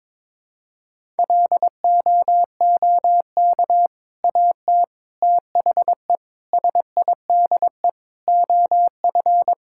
Morse code, heard as a keyed tone, about 23 wpm, 700 Hz. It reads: LOOK AT THE SIDE OF
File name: practice / 22wpm / Sentences with 3-Word Phrases - 22wpm